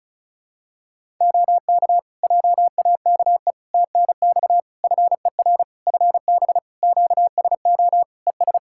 Morse code, heard as a keyed tone, about 34 wpm, 700 Hz. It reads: OK JAKE TNX FER FB QSO ES
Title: Practice QSO - Exchange 4 - Ending - 35wpm